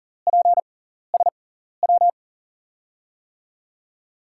Morse code, heard as a keyed tone, about 40 wpm, spaced as for 14 wpm, 700 Hz. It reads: PSW
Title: USA Call Signs - 5 Characters - 6x Character Spacing - 40wpm